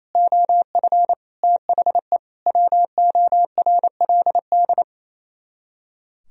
Morse code, heard as a keyed tone, about 36 wpm, 700 Hz. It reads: OF THE WORLD